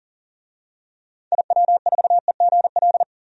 Morse code, heard as a keyed tone, about 40 wpm, 700 Hz. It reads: IW4EGL
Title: DXCC-100 Call Signs - 6 Characters - 40wpm